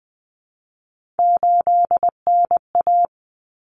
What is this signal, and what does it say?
Morse code, heard as a keyed tone, about 20 wpm, 700 Hz. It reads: 8NA
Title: Common Cut Numbers - 3-Digit 20wpm